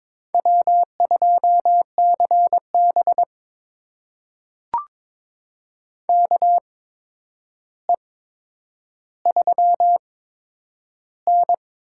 Morse code, heard as a keyed tone, about 22 wpm, spaced as for 9 wpm, 700 Hz. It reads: W2CB KE3N
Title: USA Call Signs - 4 Characters - 8x Character Spacing - 22wpm